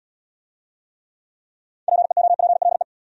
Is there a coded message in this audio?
Yes: Morse code from 1.9 to 2.8 s, heard as a keyed tone, about 68 wpm, 700 Hz. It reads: PEOPLE